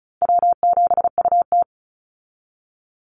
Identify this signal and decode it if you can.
Morse code, heard as a keyed tone, about 34 wpm, 700 Hz. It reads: W7UT
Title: Easy Call Signs Repeated 3x - 35wpm